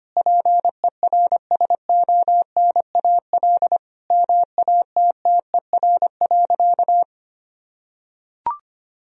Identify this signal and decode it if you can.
Morse code, heard as a keyed tone, about 25 wpm, 700 Hz. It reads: PERSONAL MATTER.
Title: Sentences from Top 1000 Words - Review Part 4 - 25wpm